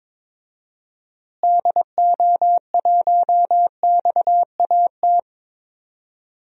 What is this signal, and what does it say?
Morse code, heard as a keyed tone, about 22 wpm, 700 Hz. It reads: DO1XAT
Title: DXCC-100 Call Signs - 6 Characters - Repeated 2x - 22wpm